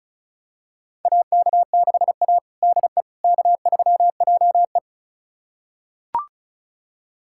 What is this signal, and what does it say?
Morse code, heard as a keyed tone, about 34 wpm, 700 Hz. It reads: AK6A DE K3JE
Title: USA Callsign DE Callsign - 4 Characters - 35wpm